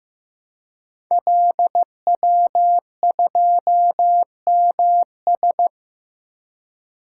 Morse code, heard as a keyed tone, about 15 wpm, 700 Hz. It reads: LW2MS
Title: Code Groups - 5 Letter:Numbers - 15wpm